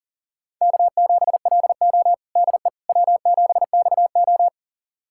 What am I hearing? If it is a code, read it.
Morse code, heard as a keyed tone, about 37 wpm, 700 Hz. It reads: K7LO DE W7XO
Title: USA Callsign DE Callsign - 4 Characters - 40wpm